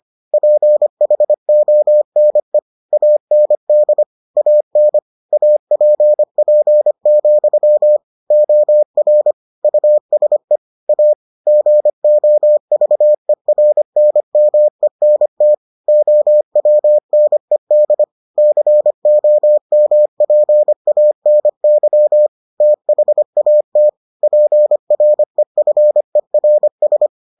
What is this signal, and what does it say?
Morse code, heard as a keyed tone, about 25 wpm, 600 Hz. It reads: PHONE AND AN APP, OR USE A GOVERNMENT OWNED COMPANY THAT PREFERS